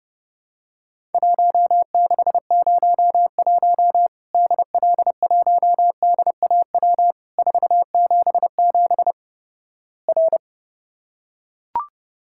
Morse code, heard as a keyed tone, about 31 wpm, 700 Hz. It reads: 1601 DL1DAW 477 R
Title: Worked All Europe DX Contest - QTC Exchange - 30wpm